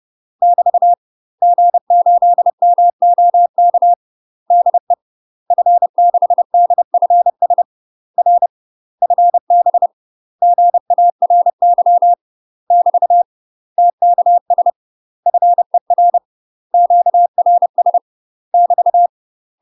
Morse code, heard as a keyed tone, about 30 wpm, 700 Hz. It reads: <BT> G8MOK DE F6DFS R FB GARY <BT> TKS FER QRS <BT>
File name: UK MASTER QSOs TRAINING / 30WPM / UK MASTER QSOs 0040